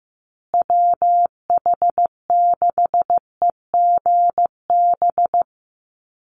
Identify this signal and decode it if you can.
Morse code, heard as a keyed tone, about 15 wpm, 700 Hz. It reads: WH6EGB
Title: USA Call Signs Repeated 2x 15wpm